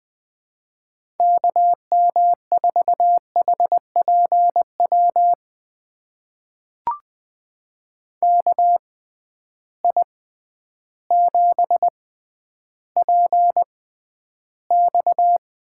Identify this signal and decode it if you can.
Morse code, heard as a keyed tone, about 20 wpm, spaced as for 11 wpm, 700 Hz. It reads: KM4HPW KI7PX
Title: USA Call Signs - 6 Characters - 6x Character Spacing - 20wpm